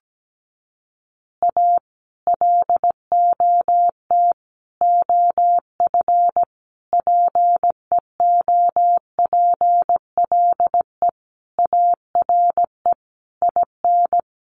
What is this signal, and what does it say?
Morse code, heard as a keyed tone, about 17 wpm, 700 Hz. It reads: A LOT OF PEOPLE ARE IN
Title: Additional Easy Phrases 17wpm